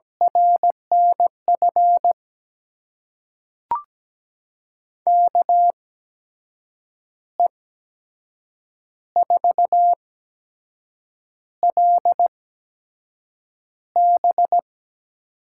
Morse code, heard as a keyed tone, about 17 wpm, spaced as for 6 wpm, 700 Hz. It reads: RNF KE4LB
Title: USA Call Signs - 6 Characters - 8x Character Spacing - 17wpm